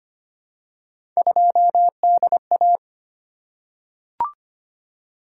Morse code, heard as a keyed tone, about 25 wpm, 700 Hz. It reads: 2DA
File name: Three Letter-Number Combinations - 25wpm